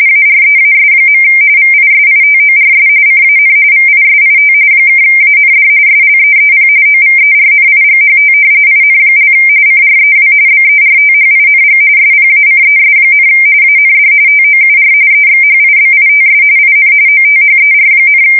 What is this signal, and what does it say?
RTTY, about 45.45 baud. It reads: AREAS. VISITORS CAN VISIT (AND SLEEP OVERNIGHT IN) CREW BERTHING SPACES, AND TOUR EXHIBITS IN OTHER,
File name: fsk170 / battleshipnj